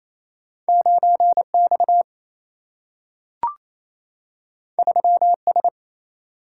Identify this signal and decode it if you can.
Morse code, heard as a keyed tone, about 28 wpm, 700 Hz. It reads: 9X 3S
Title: Single-Letter Single-Number 28wpm